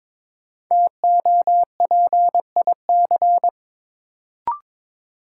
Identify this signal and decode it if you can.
Morse code, heard as a keyed tone, about 22 wpm, 700 Hz. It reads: TOPIC
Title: Sentences from Top 1200 Words - New Word Drill Alt - 22wpm